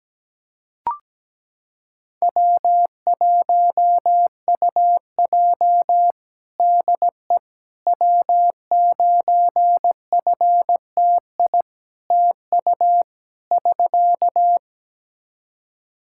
Morse code, heard as a keyed tone, about 17 wpm, 700 Hz. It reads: W1UJ DE W9FTI TU <SK>
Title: Practice QSO - Exchange 4 - Ending - 17wpm